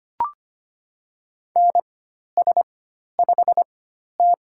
Morse code, heard as a keyed tone, about 25 wpm, spaced as for 12 wpm, 690 Hz. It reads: NS5T
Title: USA Call Signs - 4 Characters - 4x Character Spacing - 25wpm